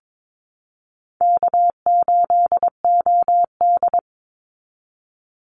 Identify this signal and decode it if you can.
Morse code, heard as a keyed tone, about 22 wpm, 700 Hz. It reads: K8OD
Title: Easy Call Signs Repeated 3x - 22wpm